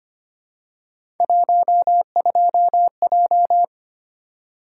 Morse code, heard as a keyed tone, about 25 wpm, 700 Hz. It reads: 12J
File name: Three Letter-Number Combinations - 25wpm